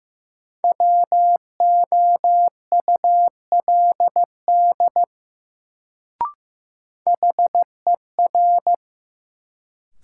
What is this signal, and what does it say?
Morse code, heard as a keyed tone, about 15 wpm, 700 Hz. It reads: WOULD HER
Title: Sentences from Top 100 Words - New Word Drill - 15wpm